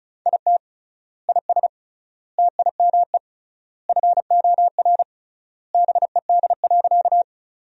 Morse code, heard as a keyed tone, about 34 wpm, 700 Hz. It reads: IT IS TIME FOR BED.
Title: 3x Word-Spacing - Sentences from Top 500 Words - 35wpm